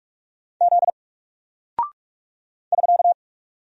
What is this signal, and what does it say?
Morse code, heard as a keyed tone, about 44 wpm, 700 Hz. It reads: Z <SK>
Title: Single Character (Letter:Number:Punctuation:Prosign) Alt 45wpm